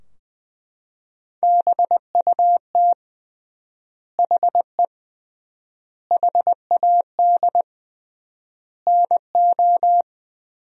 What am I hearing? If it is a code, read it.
Morse code, heard as a keyed tone, about 20 wpm, 700 Hz. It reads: BUT HE HAD NO
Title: 3x Word-Spacing - Sentences from Top 100 Words - 20wpm